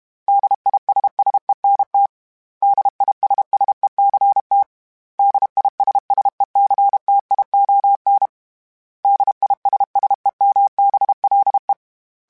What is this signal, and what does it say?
Morse code, heard as a keyed tone, about 32 wpm, 800 Hz. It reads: DISSENT DISSECT DISSECTION DISSEMBLE
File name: Special INGs-mp3s / Special INGs-mp3s / Suffixes @30